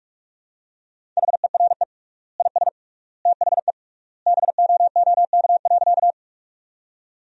Morse code, heard as a keyed tone, about 44 wpm, 700 Hz. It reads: HERE IS THE BOOK.
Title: 3x Word-Spacing - Sentences from Top 300 Words - 45wpm